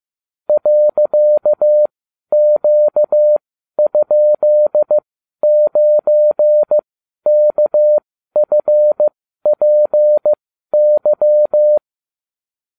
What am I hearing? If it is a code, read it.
Morse code, heard as a keyed tone, about 15 wpm, spaced as for 12 wpm, 600 Hz. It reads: .Q?9KFPY